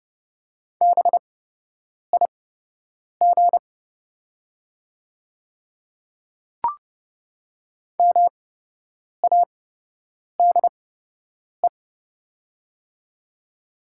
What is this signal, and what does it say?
Morse code, heard as a keyed tone, about 29 wpm, spaced as for 8 wpm, 690 Hz. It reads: BIG MADE
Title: 8x Character Spacing - Top 200 Words - 30wpm